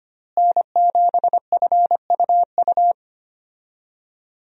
Morse code, heard as a keyed tone, about 25 wpm, 700 Hz. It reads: N7FUU
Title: USA Call Signs - 5 Characters - Repeated 3x - 25wpm